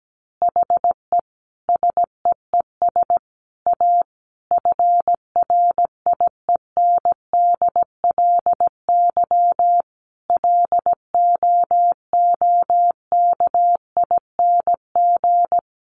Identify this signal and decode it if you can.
Morse code, heard as a keyed tone, about 17 wpm, 700 Hz. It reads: HE SEES A FRIENDLY LOOKING
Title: Easy Phrases 17wpm